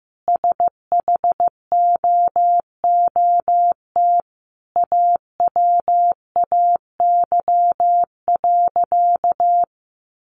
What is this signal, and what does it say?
Morse code, heard as a keyed tone, about 15 wpm, 700 Hz. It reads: SHOOT AWAY.
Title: Sentences from Top 1200 Words 15wpm